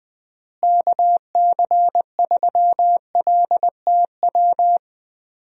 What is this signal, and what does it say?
Morse code, heard as a keyed tone, about 20 wpm, 700 Hz. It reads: KC3LTW
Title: USA Call Signs - 6 Characters - Repeated 3x - 20wpm